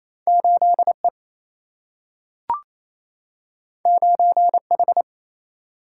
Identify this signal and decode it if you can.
Morse code, heard as a keyed tone, about 28 wpm, 700 Hz. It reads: 8E 9H
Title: Single-Letter Single-Number 28wpm